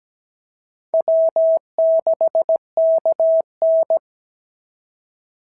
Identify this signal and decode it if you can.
Morse code, heard as a keyed tone, about 16 wpm, 650 Hz. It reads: W6KN
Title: SOTA QSO Practice 17wpm